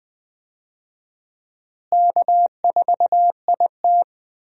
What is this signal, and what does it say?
Morse code, heard as a keyed tone, about 20 wpm, 700 Hz. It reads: K4IT
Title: USA Call Signs - 4 Characters - Repeated 3x - 20wpm